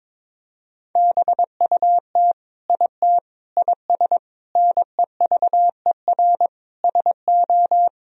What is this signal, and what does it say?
Morse code, heard as a keyed tone, about 22 wpm, 700 Hz. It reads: BUT IT IS NEVER SO
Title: Sentences from Top 1000 Words - Review Part 2 - 22wpm